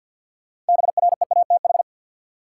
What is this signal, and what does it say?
Morse code, heard as a keyed tone, about 53 wpm, 700 Hz. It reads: BREATH
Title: Sets of 5 Words Encore - New Word Drill - 50wpm